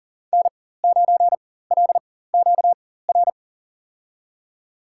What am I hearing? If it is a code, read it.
Morse code, heard as a keyed tone, about 40 wpm, spaced as for 19 wpm, 700 Hz. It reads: N9LQR
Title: USA Call Signs - 5 Characters - 4x Character Spacing - 40wpm